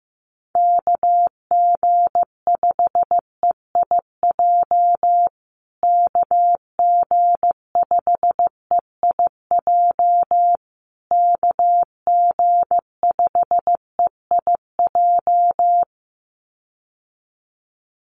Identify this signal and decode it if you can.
Morse code, heard as a keyed tone, about 15 wpm, 700 Hz. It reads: KG5EIJ KG5EIJ KG5EIJ
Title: USA Call Signs Repeated 3x 15wpm